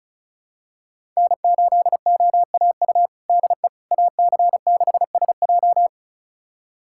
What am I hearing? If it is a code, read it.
Morse code, heard as a keyed tone, about 34 wpm, 700 Hz. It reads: N8OAU DE AC6SJ
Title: USA Callsign DE Callsign - 5 Characters - 35wpm